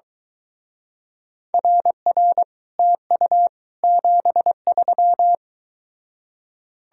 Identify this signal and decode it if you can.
Morse code, heard as a keyed tone, about 23 wpm, 700 Hz. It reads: RR TU 73